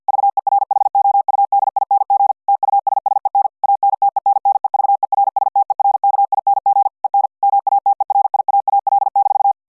Morse code, heard as a keyed tone, about 50 wpm, 800 Hz. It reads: VERHOUDING TUSSEN ANTENNEVERSTERKING EN MATERIAAL-